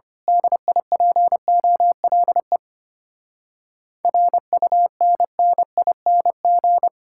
Morse code, heard as a keyed tone, about 25 wpm, 700 Hz. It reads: DIPOLE RUNNING
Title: Speed-Racing Common QSO Phrases - 25wpm